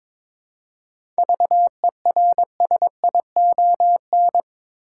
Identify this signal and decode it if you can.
Morse code, heard as a keyed tone, about 22 wpm, 700 Hz. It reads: VERSION
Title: Seven Letter Words - 22wpm